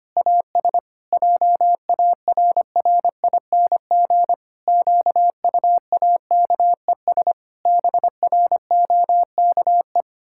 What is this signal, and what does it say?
Morse code, heard as a keyed tone, about 25 wpm, 700 Hz. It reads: AS JARRING QUAKES BROKE
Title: Pangram Challenge v2 - 25wpm